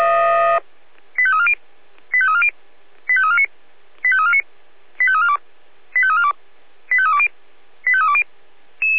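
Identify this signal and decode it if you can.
Five-tone selective call (ZVEI), heard as quick runs of five tones, tones of about 70 ms. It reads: 85340 85340 85349 85349 85342 85342 85320 85320
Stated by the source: ZVEI five-tone call received by SDR.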